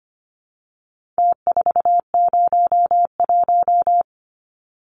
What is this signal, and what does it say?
Morse code, heard as a keyed tone, about 25 wpm, 700 Hz. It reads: T401
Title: Numbers - Contest Style - 4-Digit 25wpm